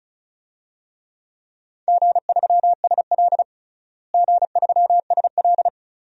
Morse code, heard as a keyed tone, about 34 wpm, 700 Hz. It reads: G3SL G3SL
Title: DXCC-100 Call Signs - 4 Characters - Repeated 2x - 35wpm